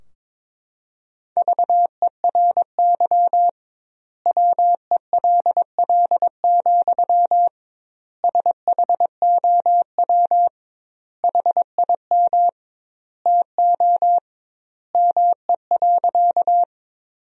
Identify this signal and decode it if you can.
Morse code, heard as a keyed tone, about 22 wpm, 700 Hz. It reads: VERY WELL, SHOW HIM TO ME.
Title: 2x Word-Spacing - Sentences from Top 200 Words - 22wpm